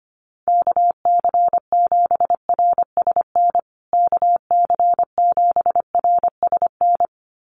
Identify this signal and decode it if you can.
Morse code, heard as a keyed tone, about 25 wpm, 700 Hz. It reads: KC7RSN KC7RSN
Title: USA Call Signs Repeated 2x 25wpm